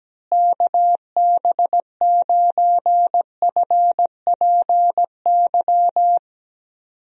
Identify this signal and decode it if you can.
Morse code, heard as a keyed tone, about 17 wpm, 700 Hz. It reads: KB9FPY
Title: USA Call Signs - 6 Characters - Repeated 3x - 17wpm